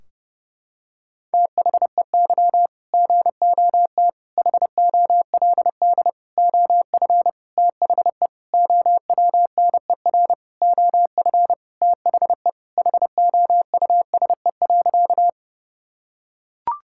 Morse code, heard as a keyed tone, about 30 wpm, 700 Hz. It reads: THEY GOT HOLD OF THE OWNER OF THE HOUSE.
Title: Sentences from Top 1600 Words 30wpm